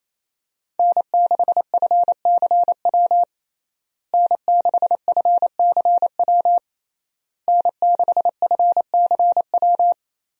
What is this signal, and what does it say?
Morse code, heard as a keyed tone, about 28 wpm, 700 Hz. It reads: N6FCW N6FCW N6FCW
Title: USA Call Signs - 5 Characters - Repeated 3x - 28wpm